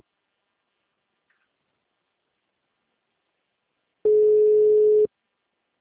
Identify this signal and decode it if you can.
Ringback tone (Europe, 425 Hz).